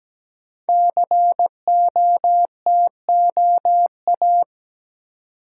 Morse code, heard as a keyed tone, about 17 wpm, 700 Hz. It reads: COTOA
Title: Code Groups - 5 Letters - 17wpm